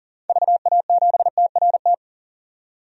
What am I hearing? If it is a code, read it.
Morse code, heard as a keyed tone, about 40 wpm, 700 Hz. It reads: VA7TRT
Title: DXCC-100 Call Signs - 6 Characters - 40wpm